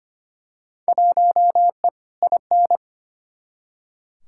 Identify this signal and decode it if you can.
Morse code, heard as a keyed tone, about 25 wpm, 700 Hz. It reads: 1E IN
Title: Field Day Exchanges - 25wpm